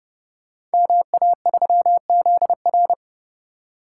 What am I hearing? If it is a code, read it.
Morse code, heard as a keyed tone, about 30 wpm, 700 Hz. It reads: MA3ZR
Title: Code Groups - 5 Letter:Numbers - 30wpm